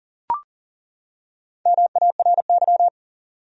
Morse code, heard as a keyed tone, about 38 wpm, 700 Hz. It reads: MARY
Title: USA Common Names - Max 4 Letters - 40wpm